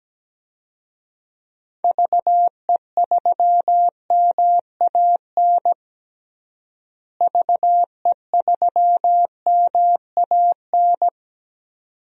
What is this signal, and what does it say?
Morse code, heard as a keyed tone, about 17 wpm, 700 Hz. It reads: VE3MAN VE3MAN
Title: DXCC-100 Call Signs - 6 Characters - Repeated 2x - 17wpm